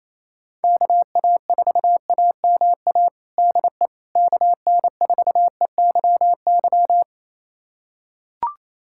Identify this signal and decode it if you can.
Morse code, heard as a keyed tone, about 27 wpm, 700 Hz. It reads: KA4AMA DE KN4EYY
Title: USA Callsign DE Callsign 28wpm